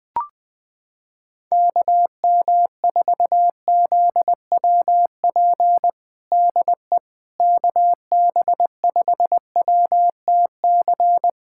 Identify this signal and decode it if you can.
Morse code, heard as a keyed tone, about 20 wpm, 700 Hz. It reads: KM4ZWP DE KB5WTC
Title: USA Callsign DE Callsign 20wpm